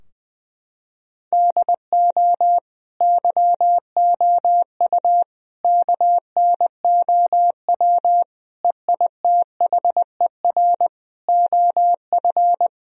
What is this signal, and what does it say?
Morse code, heard as a keyed tone, about 20 wpm, 700 Hz. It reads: DO YOU KNOW EITHER OF
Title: Sentences from Top 1000 Words - Review Part 4 - 20wpm